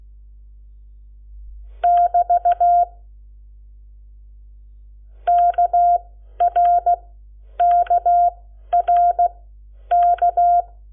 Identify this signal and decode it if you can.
Morse code, heard as a keyed tone, about 16 wpm, spaced as for 10 wpm, 700 Hz. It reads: <BT> KRKRK
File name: Lecciones / CLASE Nº7